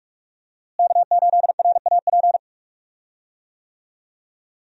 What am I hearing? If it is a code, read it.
Morse code, heard as a keyed tone, about 46 wpm, 700 Hz. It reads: K8RAP
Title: USA Call Signs - 5 Characters - Repeated 3x - 45wpm